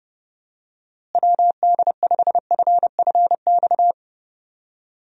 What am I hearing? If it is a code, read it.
Morse code, heard as a keyed tone, about 30 wpm, 700 Hz. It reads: WD5FFX
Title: USA Call Signs - 6 Characters - Repeated 3x - 30wpm